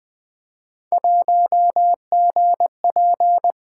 Morse code, heard as a keyed tone, about 20 wpm, 700 Hz. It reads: 1GP
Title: Three Letter-Number Combinations - 20wpm